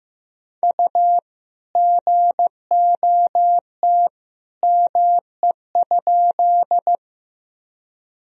Morse code, heard as a keyed tone, about 15 wpm, 700 Hz. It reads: U GOT ME?
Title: Sentences from Top 300 Words and Abbreviations 15wpm